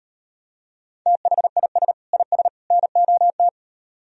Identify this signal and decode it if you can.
Morse code, heard as a keyed tone, about 38 wpm, 700 Hz. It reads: THIS IS NOT